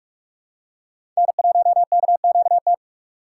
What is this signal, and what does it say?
Morse code, heard as a keyed tone, about 46 wpm, 700 Hz. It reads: N1KQT
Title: USA Call Signs - 5 Characters - 4x Character Spacing - 45wpm